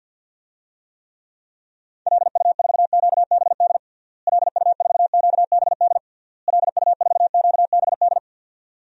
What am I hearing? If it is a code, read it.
Morse code, heard as a keyed tone, about 50 wpm, 700 Hz. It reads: LU4QBD LU4QBD LU4QBD
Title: DXCC-100 Call Signs - 6 Characters - Repeated 3x - 50wpm